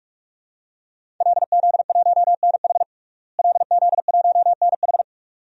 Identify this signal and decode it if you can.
Morse code, heard as a keyed tone, about 42 wpm, 700 Hz. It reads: LZ1NH LZ1NH
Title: DXCC-100 Call Signs - 5 Characters - Repeated 2x - 45wpm